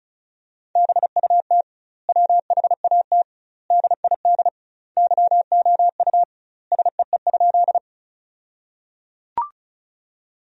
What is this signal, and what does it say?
Morse code, heard as a keyed tone, about 36 wpm, 700 Hz. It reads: BUT WHAT DID YOU SEE?
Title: 2x Word-Spacing - Sentences from Top 100 Words - 35wpm